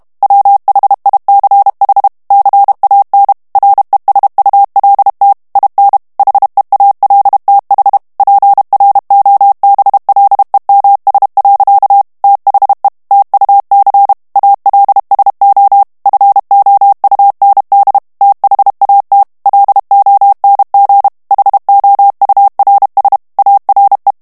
Morse code, heard as a keyed tone, about 32 wpm, 800 Hz. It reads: WHICH CAN RESULT IN HEALTH PROBLEMS. THE TUC ALSO FOUND THAT LONG HOURS ARE
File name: morse mp3 / ANGLAIS texte clair 1 vitesse 30 mots minute